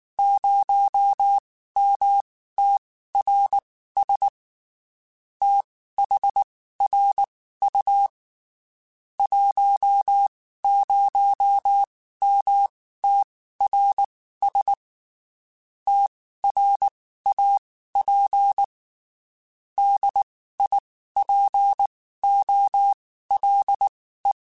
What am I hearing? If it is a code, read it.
Morse code, heard as a keyed tone, about 19 wpm, spaced as for 14 wpm, 800 Hz. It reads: 0MTRS THRU 10MTRS TRAP DIPOLE
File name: QSOs / QSOs-mp3s / WD0FBT 1@18